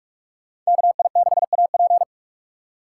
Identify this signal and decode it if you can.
Morse code, heard as a keyed tone, about 57 wpm, 690 Hz. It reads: KI6AP